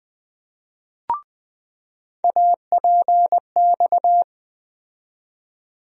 Morse code, heard as a keyed tone, about 20 wpm, 700 Hz. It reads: APX
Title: USA Call Sign Postfixes - 3 Letters 20wpm